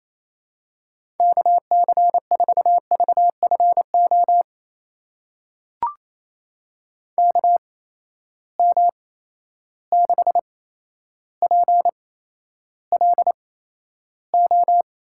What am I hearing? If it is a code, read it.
Morse code, heard as a keyed tone, about 28 wpm, spaced as for 12 wpm, 700 Hz. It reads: KC4VFO KM6PLO
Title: USA Call Signs - 8x Character Spacing - 28wpm